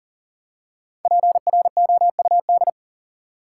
Morse code, heard as a keyed tone, about 40 wpm, 700 Hz. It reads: PROUD
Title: Five Letter Words - 40wpm